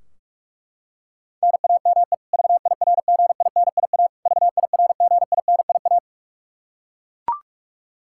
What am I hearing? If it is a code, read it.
Morse code, heard as a keyed tone, about 41 wpm, 700 Hz. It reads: NAME VIRGINIA VIRGINIA
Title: Practice QSO - Exchange 1 - Beginning - 45wpm